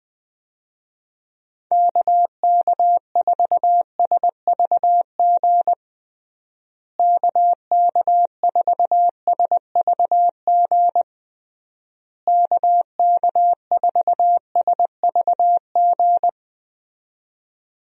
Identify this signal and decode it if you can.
Morse code, heard as a keyed tone, about 20 wpm, 700 Hz. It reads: KK4SVG KK4SVG KK4SVG
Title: USA Call Signs - 6 Characters - Repeated 3x - 20wpm